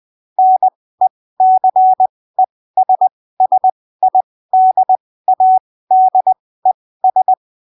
Morse code, heard as a keyed tone, about 20 wpm, spaced as for 15 wpm, 750 Hz. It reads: NECESSIDADES
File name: arquivos / texto / texto 23 20ppm